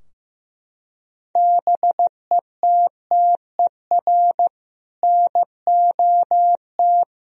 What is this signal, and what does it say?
Morse code, heard as a keyed tone, about 15 wpm, 700 Hz. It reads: BETTER NOT
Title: Sentences from Top 1000 Words - Review Part 2 - 15wpm